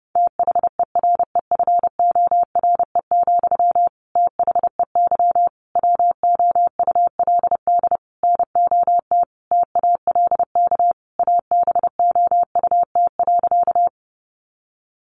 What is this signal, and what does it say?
Morse code, heard as a keyed tone, about 30 wpm, 700 Hz. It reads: THEREFORE, THEY WOULD NOT TALK ABOUT.
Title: Sentences from Top 800 Words 30wpm